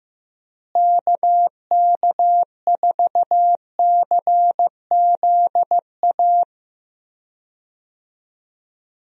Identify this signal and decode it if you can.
Morse code, heard as a keyed tone, about 15 wpm, 700 Hz. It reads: KK4CZA
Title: USA Call Signs - 6 Characters - Repeated 3x - 15wpm